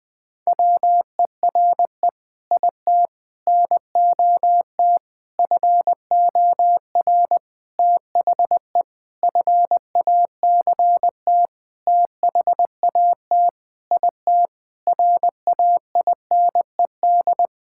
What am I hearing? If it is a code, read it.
Morse code, heard as a keyed tone, about 20 wpm, 700 Hz. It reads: WERE IT NOT FOR THE FACT THAT IT RAINED
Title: Sentences with 7-Word Phrases - 20wpm